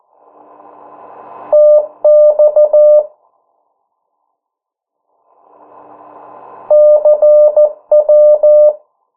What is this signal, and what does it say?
Morse code, heard as a keyed tone, about 14 wpm, 600 Hz. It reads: TX CW
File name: morse mp3 / 09 urok 8